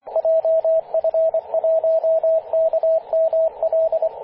Morse code, heard as a keyed tone, about 25 wpm, 650 Hz, with radio noise. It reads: JF1KML
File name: posts / 2016-apr-23-0243z-t88fg-18-085mhz-cw